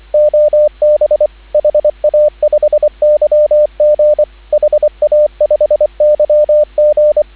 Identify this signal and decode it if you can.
Morse code, heard as a keyed tone, about 25 wpm, 600 Hz, with radio noise. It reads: OB HA5YG HA5YG
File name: CW / 2017 HA-OB HA5YG